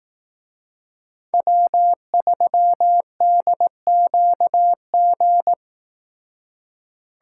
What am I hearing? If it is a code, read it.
Morse code, heard as a keyed tone, about 18 wpm, 700 Hz. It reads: W3DQG
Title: Speed Racing - USA 1x3 Call Signs 15wpm